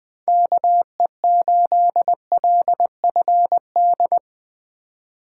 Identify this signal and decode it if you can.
Morse code, heard as a keyed tone, about 20 wpm, 700 Hz. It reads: KE8LFD